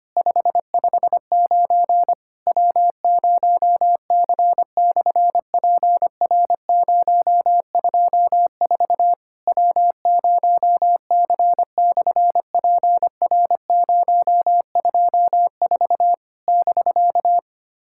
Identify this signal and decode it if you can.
Morse code, heard as a keyed tone, about 25 wpm, 700 Hz. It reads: 559 W0C/PR024 W0C/PR024 <BK>
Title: SOTA QSO Practice 25wpm